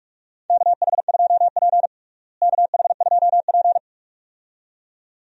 Morse code, heard as a keyed tone, about 44 wpm, 700 Hz. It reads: KH2P KH2P
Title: USA Call Signs - 4 Characters - Repeated 2x - 45wpm